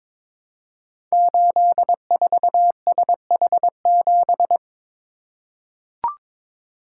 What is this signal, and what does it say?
Morse code, heard as a keyed tone, about 22 wpm, 700 Hz. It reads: 84SH7
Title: Code Groups - 5 Letter:Numbers - 22wpm